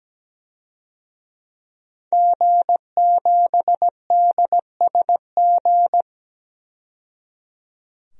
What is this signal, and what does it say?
Morse code, heard as a keyed tone, about 17 wpm, 700 Hz. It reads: G7DSG
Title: DXCC-100 Call Signs - 5 Characters - 17wpm